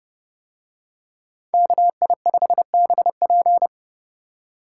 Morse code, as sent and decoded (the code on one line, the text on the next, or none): -.- .. ..... -... .--.
KI5BP